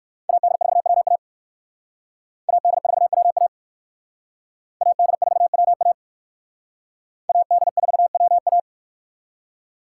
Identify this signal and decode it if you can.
Morse code, heard as a keyed tone, about 58 wpm, 700 Hz. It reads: AD4WA AD4WA AD4WA AD4WA